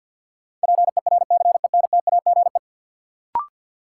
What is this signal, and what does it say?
Morse code, heard as a keyed tone, about 53 wpm, 700 Hz. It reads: PERCENTAGE
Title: Sentences from Top 1300 Words - New Word Drill Alt - 50wpm